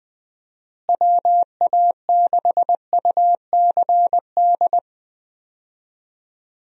Morse code, heard as a keyed tone, about 20 wpm, 700 Hz. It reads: WA6UCD
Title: USA Call Signs - 6 Characters - Repeated 3x - 20wpm